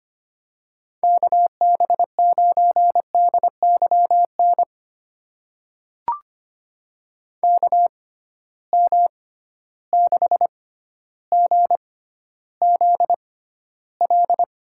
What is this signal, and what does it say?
Morse code, heard as a keyed tone, about 25 wpm, spaced as for 13 wpm, 700 Hz. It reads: KB9DYN KM6GZL
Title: USA Call Signs - 6 Characters - 6x Character Spacing - 25wpm